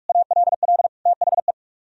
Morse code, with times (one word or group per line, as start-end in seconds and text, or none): 0.1-0.9 ALL
1.1-1.5 THE